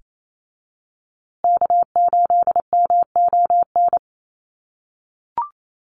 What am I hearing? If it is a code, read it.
Morse code, heard as a keyed tone, about 28 wpm, 700 Hz. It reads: K8MON
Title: USA Call Signs Repeated 2x 28wpm